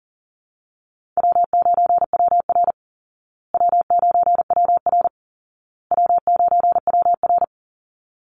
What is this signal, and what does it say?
Morse code, heard as a keyed tone, about 40 wpm, 700 Hz. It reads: W9WR W9WR W9WR
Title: Easy Call Signs Repeated 3x - 40wpm